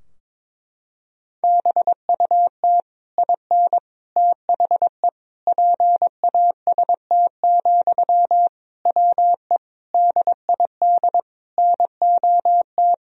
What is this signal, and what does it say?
Morse code, heard as a keyed tone, about 22 wpm, 700 Hz. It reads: BUT IN THE PAST, WE DID NOT
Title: Sentences from Top 1000 Words - Review Part 3 - 22wpm